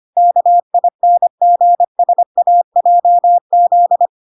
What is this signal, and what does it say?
Morse code, heard as a keyed tone, about 25 wpm, 700 Hz. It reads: KINGSAJZ